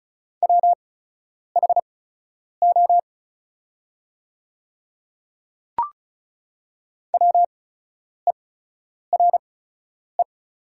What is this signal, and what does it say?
Morse code, heard as a keyed tone, about 36 wpm, spaced as for 10 wpm, 690 Hz. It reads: WHO WERE
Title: 8x Character Spacing - Top 100 Words - 35wpm